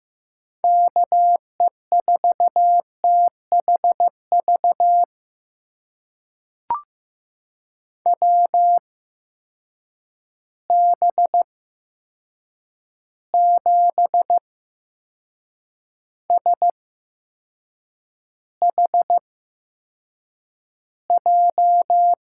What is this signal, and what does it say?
Morse code, heard as a keyed tone, about 15 wpm, spaced as for 6 wpm, 700 Hz. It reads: KE4THV WB7SHJ
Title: USA Call Signs - 6 Characters - 8x Character Spacing - 15wpm